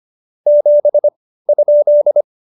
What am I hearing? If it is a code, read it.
Morse code, heard as a keyed tone, about 25 wpm, spaced as for 15 wpm, 590 Hz. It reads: 7?
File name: Jokes-25-char-15-eff